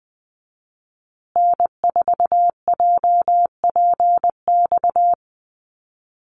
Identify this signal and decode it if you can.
Morse code, heard as a keyed tone, about 20 wpm, 700 Hz. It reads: N4JPX
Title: USA Call Signs Repeated 3x 20wpm